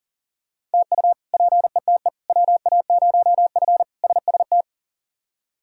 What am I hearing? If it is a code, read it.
Morse code, heard as a keyed tone, about 37 wpm, 690 Hz. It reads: TU PETE WA0F SST
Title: K1USN SST QSO Practice - 40wpm